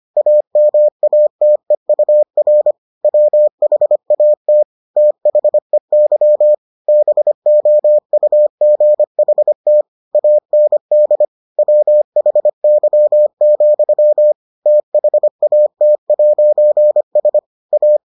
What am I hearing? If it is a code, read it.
Morse code, heard as a keyed tone, about 25 wpm, 600 Hz. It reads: AMATEUR WHAT THEY BOUGHT AND WHY, THAT'S A